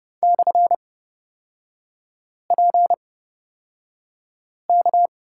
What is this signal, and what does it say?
Morse code, heard as a keyed tone, about 30 wpm, spaced as for 5 wpm, 690 Hz. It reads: /PK
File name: course / 30:10wpm / 236 Lesson 38e - Letter q - Three Characters 30wpm